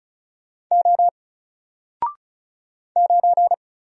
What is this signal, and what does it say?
Morse code, heard as a keyed tone, about 33 wpm, 700 Hz. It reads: O 9
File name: Single-Letter-Number-35wpm